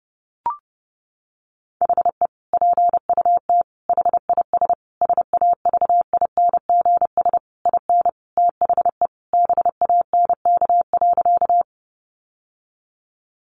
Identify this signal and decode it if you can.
Morse code, heard as a keyed tone, about 30 wpm, 700 Hz. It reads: HE PUT HIS SAVINGS IN THE BANK.
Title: Sentences from Top 1200 Words 30wpm